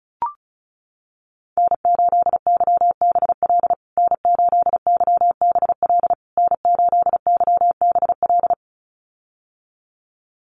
Morse code, heard as a keyed tone, about 34 wpm, 700 Hz. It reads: N8YBL N8YBL N8YBL
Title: USA Call Signs Repeated 3x 35wpm